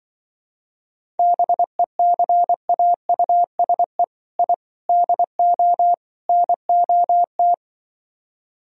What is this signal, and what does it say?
Morse code, heard as a keyed tone, about 24 wpm, 700 Hz. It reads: BECAUSE I DO NOT
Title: Speed-Racing - Sets of 4 Words 20wpm